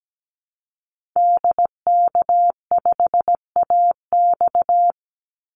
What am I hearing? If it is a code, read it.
Morse code, heard as a keyed tone, about 17 wpm, 700 Hz. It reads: DK5AX
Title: Call Signs Repeated 3x - 17wpm